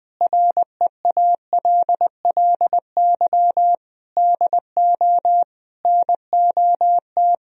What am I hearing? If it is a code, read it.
Morse code, heard as a keyed tone, about 20 wpm, 700 Hz. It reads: REALLY DO NOT
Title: Speed-Racing - Sets of 4 Words 17wpm